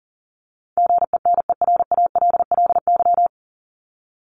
Morse code, heard as a keyed tone, about 40 wpm, 700 Hz. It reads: GENERALLY
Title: LY Suffix - Top 100 Words with Base Words 40wpm